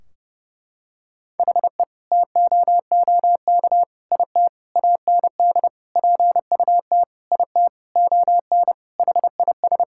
Morse code, heard as a keyed tone, about 30 wpm, 700 Hz. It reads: HE TOOK IT AND PUT IT ON HIS
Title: Sentences from Top 1600 Words 30wpm